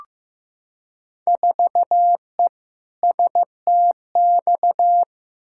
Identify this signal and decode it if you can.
Morse code, heard as a keyed tone, about 15 wpm, 700 Hz. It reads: 4E STX
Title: Field Day Exchanges Encore - 15wpm